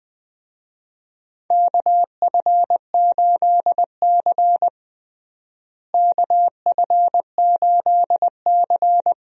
Morse code, heard as a keyed tone, about 20 wpm, 700 Hz. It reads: KF8C KF8C
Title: USA Call Signs - 4 Characters - Repeated 3x - 20wpm